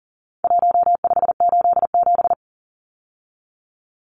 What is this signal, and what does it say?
Morse code, heard as a keyed tone, about 40 wpm, 700 Hz. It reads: 1587
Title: Numbers - Contest Style - 4-Digit 40wpm